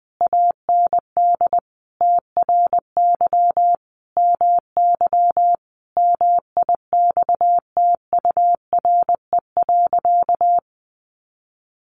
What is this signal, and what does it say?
Morse code, heard as a keyed tone, about 20 wpm, 700 Hz. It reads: AND TRY MY MIXTURE.
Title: Sentences from Top 1500 Words 20wpm